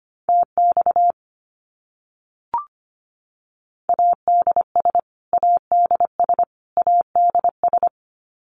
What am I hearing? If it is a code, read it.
Morse code, heard as a keyed tone, about 25 wpm, 700 Hz. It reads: TX ADS ADS ADS
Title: Abbreviations Repeated 3x - 25wpm